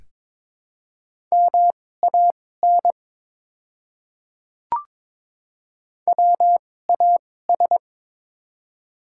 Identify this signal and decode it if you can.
Morse code, heard as a keyed tone, about 22 wpm, spaced as for 16 wpm, 700 Hz. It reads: MAN WAS
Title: 2x Character Spacing - Top 200 Words - 22wpm